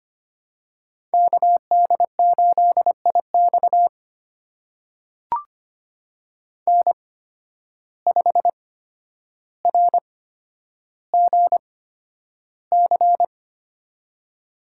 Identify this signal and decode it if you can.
Morse code, heard as a keyed tone, about 25 wpm, spaced as for 11 wpm, 700 Hz. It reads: KD8IX N5RGC
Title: USA Call Signs - 5 Characters - 8x Character Spacing - 25wpm